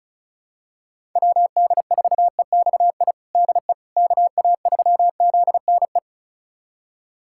Morse code, heard as a keyed tone, about 36 wpm, 690 Hz. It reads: WD4EXI DE KA3ZNE
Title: USA Callsign DE Callsign - 6 Characters - 35wpm